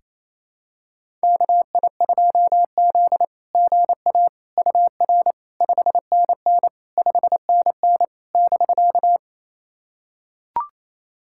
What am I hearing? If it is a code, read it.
Morse code, heard as a keyed tone, about 28 wpm, 690 Hz. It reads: KI2Z GA UR 5NN 5NN <BK>
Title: SOTA QSO Practice 28wpm